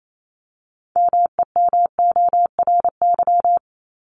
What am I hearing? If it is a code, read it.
Morse code, heard as a keyed tone, about 28 wpm, 700 Hz. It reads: MEMORY
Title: Top 1251-1500 Words 28wpm